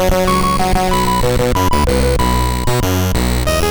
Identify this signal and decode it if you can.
There is no code.